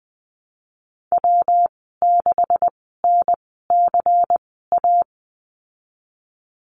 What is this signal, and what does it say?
Morse code, heard as a keyed tone, about 20 wpm, spaced as for 15 wpm, 700 Hz. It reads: W6NCA
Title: 147 Lesson 24g - Letter y - Call Signs 20wpm